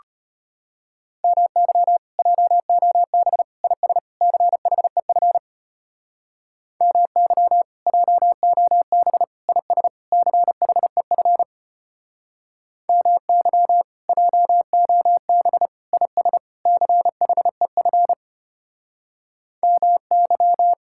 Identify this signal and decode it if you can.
Morse code, heard as a keyed tone, about 32 wpm, 700 Hz. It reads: MY JOB IS CHEF MY JOB IS CHEF MY JOB IS CHEF MY